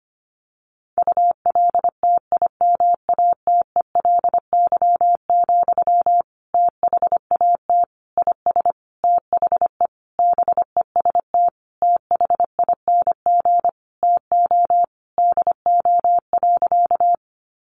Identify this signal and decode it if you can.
Morse code, heard as a keyed tone, about 25 wpm, 700 Hz. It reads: ULTIMATELY, THAT IS THE BEST THING TO DO.
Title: Sentences from Top 1200 Words 25wpm